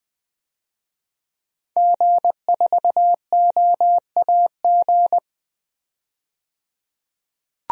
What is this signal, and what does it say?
Morse code, heard as a keyed tone, about 21 wpm, 700 Hz. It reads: G4OAG
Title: DXCC-100 Call Signs - 5 Characters - 20wpm